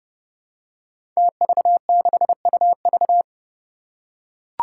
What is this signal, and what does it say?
Morse code, heard as a keyed tone, about 30 wpm, 700 Hz. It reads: TV6UV
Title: Five Challenging Letter-Number Combinations 30wpm